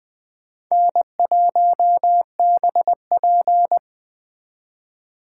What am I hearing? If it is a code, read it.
Morse code, heard as a keyed tone, about 23 wpm, 700 Hz. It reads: N1BP
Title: Speed Racing - USA 1x2 Call Signs 20wpm